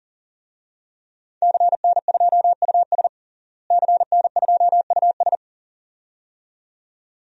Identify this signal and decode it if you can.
Morse code, heard as a keyed tone, about 40 wpm, 700 Hz. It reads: CN2US CN2US
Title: DXCC-100 Call Signs - 5 Characters - Repeated 2x - 40wpm